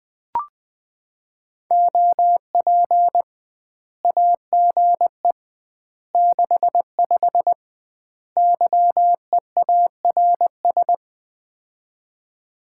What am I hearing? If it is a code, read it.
Morse code, heard as a keyed tone, about 20 wpm, 700 Hz. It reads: OP AGE 65 YEARS
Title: Common QSO Phrases - 2x Word-Spacing - 20wpm